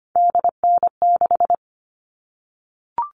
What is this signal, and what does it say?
Morse code, heard as a keyed tone, about 25 wpm, 700 Hz. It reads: DN6
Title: All Cut Numbers - 3-Digit 25wpm